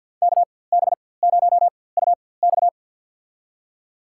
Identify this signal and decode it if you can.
Morse code, heard as a keyed tone, about 51 wpm, spaced as for 23 wpm, 700 Hz. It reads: KB0UX
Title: USA Call Signs - 5 Characters - 4x Character Spacing - 50wpm